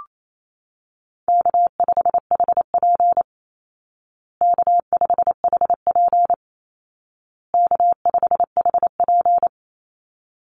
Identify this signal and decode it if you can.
Morse code, heard as a keyed tone, about 28 wpm, 700 Hz. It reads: K5HP K5HP K5HP
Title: Call Signs Repeated 2x - 28wpm